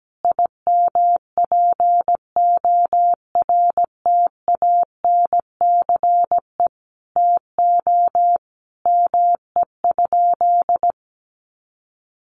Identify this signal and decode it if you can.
Morse code, heard as a keyed tone, about 17 wpm, 700 Hz. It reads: IMPORTANCE TO ME?
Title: Sentences from Top 1200 Words 17wpm